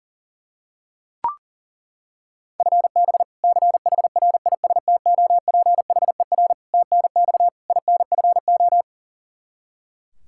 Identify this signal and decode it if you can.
Morse code, heard as a keyed tone, about 37 wpm, 700 Hz. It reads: FB CHRISTOPHER TNX INFO
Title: K1USN SST QSO Practice 40wpm